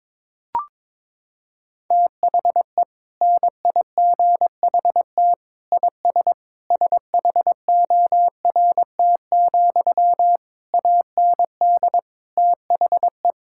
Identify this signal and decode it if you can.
Morse code, heard as a keyed tone, about 22 wpm, 700 Hz. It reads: THE NIGHT IS SHORT, AND THE
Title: Sentences from Top 1600 Words 22wpm